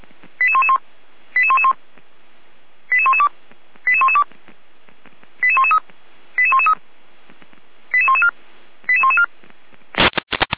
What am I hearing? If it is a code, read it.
Five-tone selective call (ZVEI), heard as quick runs of five tones, tones of about 70 ms. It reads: 88111 88111 88112 88112 88113 88113 88114 88114
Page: audio.f85360.feuerwehr-orlamuende.org